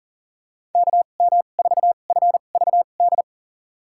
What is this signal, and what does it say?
Morse code, heard as a keyed tone, about 37 wpm, spaced as for 28 wpm, 700 Hz. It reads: KM4FVD
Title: USA Call Signs - 6 Characters - 2x Character Spacing - 40wpm